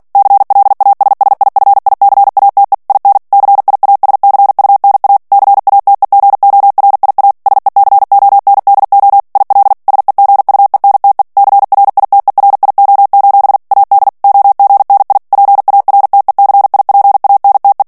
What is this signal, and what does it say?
Morse code, heard as a keyed tone, about 48 wpm, 800 Hz. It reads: CLASSIFICATE IN CIASCUNA CATEGORIA SECONDO IL SEGUENTE CRITERIO: AD OGNI PARTECIPANTE
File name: ITALIEN texte clair 2 vitesse 45 mots minute